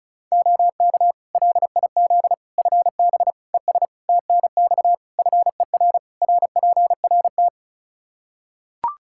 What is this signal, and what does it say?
Morse code, heard as a keyed tone, about 34 wpm, 700 Hz. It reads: OK LIZ FB ES TNX FER RPRT
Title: K1USN SST QSO Practice 35wpm